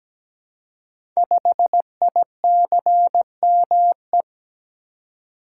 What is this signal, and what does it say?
Morse code, heard as a keyed tone, about 17 wpm, 700 Hz. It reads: 5ICME
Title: Code Groups - 5 Letter:Numbers - 17wpm